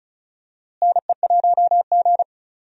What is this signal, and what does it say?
Morse code, heard as a keyed tone, about 36 wpm, 700 Hz. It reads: NE1G
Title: USA Call Signs - 4 Characters - Repeated 2x - 35wpm